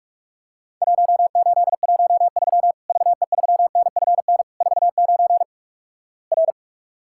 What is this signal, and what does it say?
Morse code, heard as a keyed tone, about 44 wpm, 700 Hz. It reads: 1813 VE3NFN 49 R
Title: Worked All Europe DX Contest - QTC Exchange - 45wpm